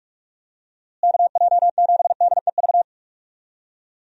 Morse code, heard as a keyed tone, about 46 wpm, 700 Hz. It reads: KJ7DEV
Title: USA Call Signs - 6 Characters - Repeated 2x - 45wpm